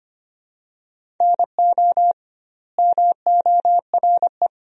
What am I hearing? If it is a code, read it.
Morse code, heard as a keyed tone, about 25 wpm, 700 Hz. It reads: NO MORE
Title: 2x Word-Spacing - Sentences from Top 400 Words - 25wpm